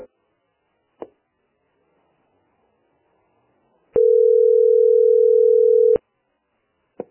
Ringback tone (North America), 440 + 480 Hz.